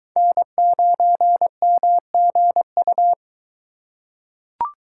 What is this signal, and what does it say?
Morse code, heard as a keyed tone, about 21 wpm, 700 Hz. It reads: N9MGU